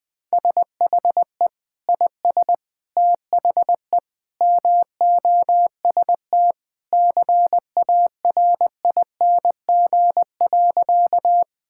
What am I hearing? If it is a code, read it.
Morse code, heard as a keyed tone, about 21 wpm, 700 Hz. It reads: SHE IS THE MOST CARING.
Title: Sentences with 2-Word Phrases - 20wpm